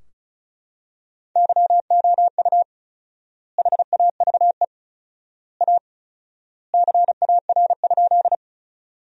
Morse code, heard as a keyed tone, about 36 wpm, 700 Hz. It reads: YOU HAVE A CAR?
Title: 4x Word-Spacing - Sentences from Top 300 Words - 35wpm